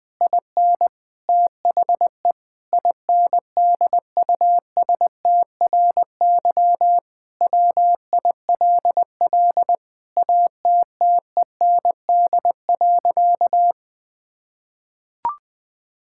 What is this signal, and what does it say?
Morse code, heard as a keyed tone, about 21 wpm, 700 Hz. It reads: IN THE INDUSTRY WILL ATTEND.
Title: Sentences with 7-Word Phrases - 20wpm